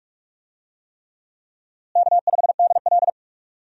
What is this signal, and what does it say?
Morse code, heard as a keyed tone, about 46 wpm, 700 Hz. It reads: K5DL
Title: USA Call Signs - 4 Characters - Repeated 3x - 45wpm